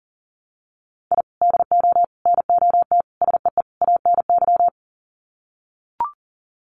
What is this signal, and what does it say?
Morse code, heard as a keyed tone, about 40 wpm, 700 Hz. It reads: I DO NOT SEE ANY
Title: Sets of 5 Words 40wpm